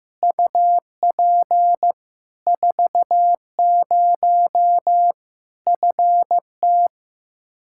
Morse code, heard as a keyed tone, about 15 wpm, 700 Hz. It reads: UP 40 FT
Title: Practice QSO - Exchange 2 - Info - 15wpm